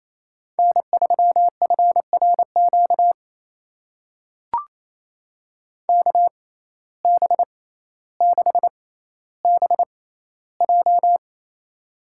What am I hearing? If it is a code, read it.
Morse code, heard as a keyed tone, about 28 wpm, spaced as for 14 wpm, 700 Hz. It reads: N3FRQ KB6BJ
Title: USA Call Signs - 6x Character Spacing - 28wpm